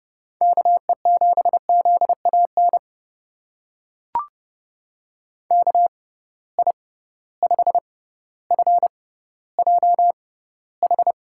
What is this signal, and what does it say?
Morse code, heard as a keyed tone, about 30 wpm, spaced as for 15 wpm, 700 Hz. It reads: KE7ZAN KI5FJH
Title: USA Call Signs - 6 Characters - 6x Character Spacing - 30wpm